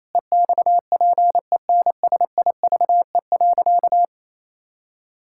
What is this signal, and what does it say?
Morse code, heard as a keyed tone, about 28 wpm, 700 Hz. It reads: EXPENSIVE.
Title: Sentences from Top 1000 Words - Review Part 6 - 28wpm